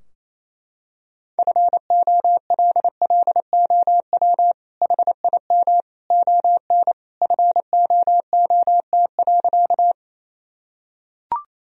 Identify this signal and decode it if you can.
Morse code, heard as a keyed tone, about 28 wpm, 700 Hz. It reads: FOLLOW HIM ON FOOT.
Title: Sentences from Top 1000 Words - Review Part 3 - 28wpm